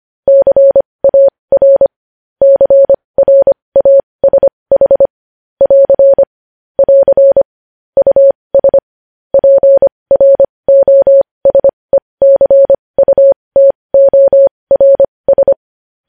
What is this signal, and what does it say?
Morse code, heard as a keyed tone, about 25 wpm, spaced as for 20 wpm, 550 Hz. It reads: CAR CRASH <AR> <AR> US PROSECUTORS
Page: news-headlines-in-morse-code-at-20-wpm